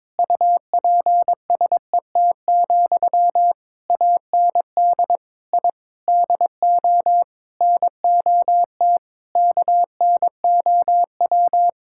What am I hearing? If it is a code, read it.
Morse code, heard as a keyed tone, about 22 wpm, 700 Hz. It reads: UPSET, AND I DO NOT KNOW
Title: Sentences from Top 1000 Words - Review Part 5 - 22wpm